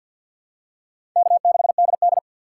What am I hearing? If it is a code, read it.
Morse code, heard as a keyed tone, about 51 wpm, 700 Hz. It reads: K6DD
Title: USA Call Signs - 4 Characters - 3x Character Spacing - 50wpm